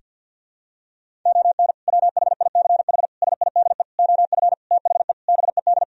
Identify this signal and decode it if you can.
Morse code, heard as a keyed tone, about 49 wpm, 700 Hz. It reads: ON WHICH SIDE OF THE BED
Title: Sentences from Top 1000 Words - Review Part 1 - 50wpm